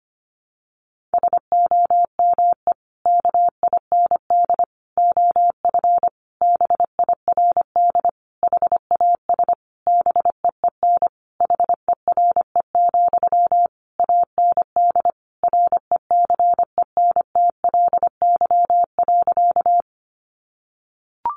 Morse code, heard as a keyed tone, about 25 wpm, 700 Hz. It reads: SOME KIND OF BIRD HAS BEEN HERE, AND RECENTLY.
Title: Sentences from Top 1200 Words 25wpm